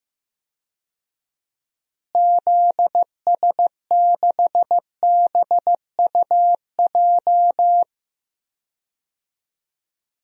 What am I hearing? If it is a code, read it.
Morse code, heard as a keyed tone, about 15 wpm, 700 Hz. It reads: ZS6BUJ
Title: DXCC-100 Call Signs - 6 Characters - Repeated 3x - 15wpm